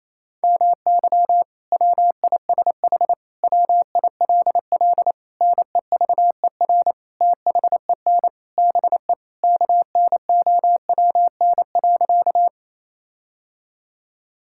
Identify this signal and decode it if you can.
Morse code, heard as a keyed tone, about 28 wpm, 700 Hz. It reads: MY WISH WILL NEVER THEN BE KNOWN.
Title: Sentences from Top 1600 Words 28wpm